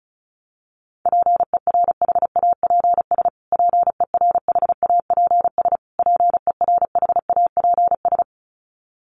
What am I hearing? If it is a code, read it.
Morse code, heard as a keyed tone, about 34 wpm, 700 Hz. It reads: PERHAPS PERHAPS PERHAPS
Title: Top 500 Words Repeated 35wpm